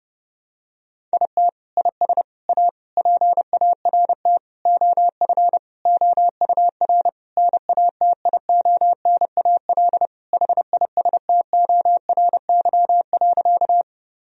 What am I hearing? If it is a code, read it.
Morse code, heard as a keyed tone, about 30 wpm, 700 Hz. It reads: IT IS A PART OF OUR NATIONAL HISTORY.
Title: Sentences from Top 1000 Words - Review Part 4 - 30wpm